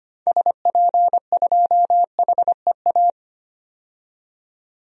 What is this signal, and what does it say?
Morse code, heard as a keyed tone, about 25 wpm, 700 Hz. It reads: SP2HEA
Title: DXCC-100 Call Signs - 6 Characters - 25wpm